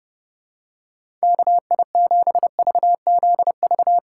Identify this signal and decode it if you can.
Morse code, heard as a keyed tone, about 30 wpm, 700 Hz. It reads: KI7VZV
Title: USA Call Signs - 2x Character Spacing - 30wpm